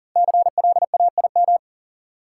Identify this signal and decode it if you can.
Morse code, heard as a keyed tone, about 40 wpm, 700 Hz. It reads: CLAIM
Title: Five Letter Words - 40wpm